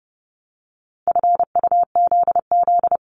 Morse code, heard as a keyed tone, about 30 wpm, 700 Hz. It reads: FUZZ
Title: Multi-letter Phonemes 30wpm